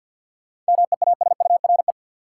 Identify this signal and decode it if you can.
Morse code, heard as a keyed tone, about 52 wpm, 700 Hz. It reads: MEASURE
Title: Sentences from Top 1000 Words - Review Part 1 - New Word Drill - 50wpm